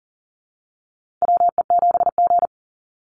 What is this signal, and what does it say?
Morse code, heard as a keyed tone, about 40 wpm, 700 Hz. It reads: WE7G
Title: Call Signs Repeated 3x - 40wpm